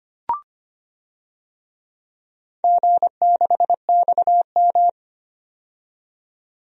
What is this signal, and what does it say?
Morse code, heard as a keyed tone, about 25 wpm, 700 Hz. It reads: G6XM
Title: DXCC-100 Call Signs - 4 Characters - 25wpm